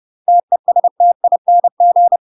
Morse code, heard as a keyed tone, about 30 wpm, 700 Hz. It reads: TESTING
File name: UK MASTER QSOs TRAINING / 30WPM / UK MASTER QSOs 0011